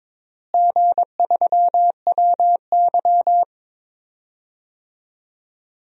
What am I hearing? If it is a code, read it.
Morse code, heard as a keyed tone, about 22 wpm, 700 Hz. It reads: G3WY
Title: DXCC-100 Call Signs - 4 Characters - Repeated 3x - 22wpm